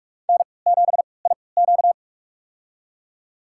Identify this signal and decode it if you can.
Morse code, heard as a keyed tone, about 44 wpm, spaced as for 25 wpm, 690 Hz. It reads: N7IQ
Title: USA Call Signs - 4 Characters - 3x Character Spacing - 45wpm